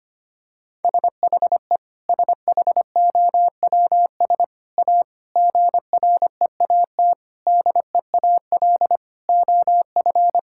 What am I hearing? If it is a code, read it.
Morse code, heard as a keyed tone, about 25 wpm, 700 Hz. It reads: SHE SHOWS A GREAT DEAL OF
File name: practice / 25wpm / Sentences with 4-Word Phrases - 25wpm